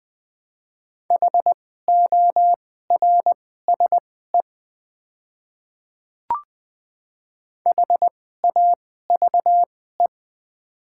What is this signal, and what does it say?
Morse code, heard as a keyed tone, about 20 wpm, spaced as for 14 wpm, 700 Hz. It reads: HORSE HAVE
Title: 2x Character Spacing - Top 300 Words - 20wpm